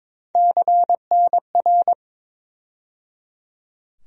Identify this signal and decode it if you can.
Morse code, heard as a keyed tone, about 22 wpm, 700 Hz. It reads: CNR
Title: USA Call Sign Postfixes - 3 Letters 22wpm